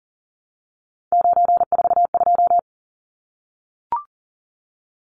Morse code, heard as a keyed tone, about 40 wpm, 700 Hz. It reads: 942
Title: Numbers - 3-Digit 40wpm